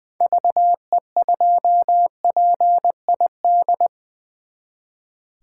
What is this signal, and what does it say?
Morse code, heard as a keyed tone, about 20 wpm, 700 Hz. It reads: VE2PID
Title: 2023 SOTA Chaser Call Signs - 20wpm